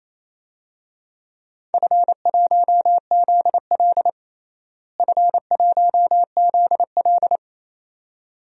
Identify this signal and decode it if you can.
Morse code, heard as a keyed tone, about 28 wpm, 700 Hz. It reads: F1ZL F1ZL
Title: DXCC-100 Call Signs - 4 Characters - Repeated 2x - 28wpm